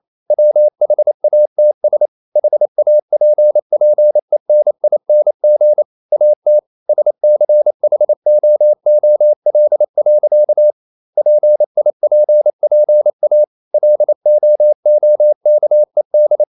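Morse code, heard as a keyed tone, about 28 wpm, 600 Hz. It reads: WHATS HAPPENING AT SCHOOL. PIPPA LOOKED
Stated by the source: Morse code practice at 28 wpm, text of Ken Follett's novel Never in English, from CW med Gnister.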